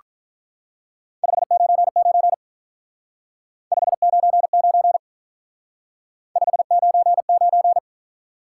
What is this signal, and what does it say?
Morse code, heard as a keyed tone, about 44 wpm, 700 Hz. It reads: 599 599 599